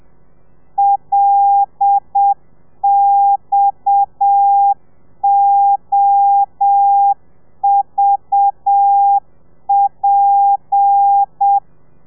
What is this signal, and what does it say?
Morse code, heard as a keyed tone, about 7 wpm, 800 Hz. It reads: LXOVP
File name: telegrafia / i2tfj / Lez7 30